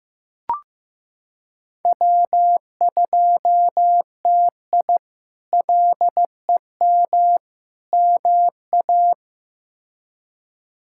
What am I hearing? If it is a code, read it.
Morse code, heard as a keyed tone, about 16 wpm, 700 Hz. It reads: W2TI LEM MA
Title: K1USN SST QSO Practice - 15wpm